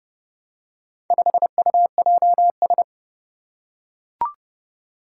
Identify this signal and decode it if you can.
Morse code, heard as a keyed tone, about 32 wpm, 700 Hz. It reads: 5UJS
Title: Four Challenging Letter-Number Combinations 30wpm